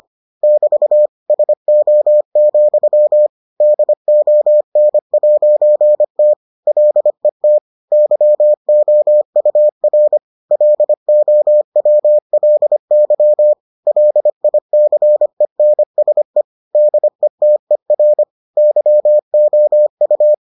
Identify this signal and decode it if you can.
Morse code, heard as a keyed tone, about 25 wpm, 600 Hz. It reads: <BT> SO, DON'T LET YOUR LOWLY LICENSE DETER YOU